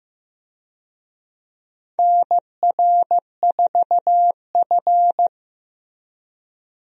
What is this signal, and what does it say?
Morse code, heard as a keyed tone, about 15 wpm, 700 Hz. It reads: NR4F
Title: USA Call Signs - 4 Characters - Repeated 2x - 15wpm